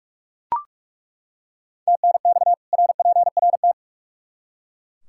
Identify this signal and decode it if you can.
Morse code, heard as a keyed tone, about 41 wpm, 700 Hz. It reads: TNX RPRT
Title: Practice QSO - Exchange 2 - Info - 45wpm